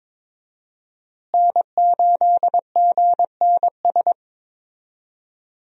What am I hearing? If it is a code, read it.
Morse code, heard as a keyed tone, about 22 wpm, 700 Hz. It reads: N8GNS
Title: USA Call Signs - 5 Characters - Repeated 2x - 22wpm